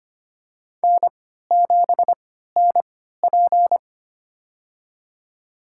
Morse code, heard as a keyed tone, about 25 wpm, spaced as for 14 wpm, 700 Hz. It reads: N7NP
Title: USA Call Signs - 4 Characters - 3x Character Spacing - 25wpm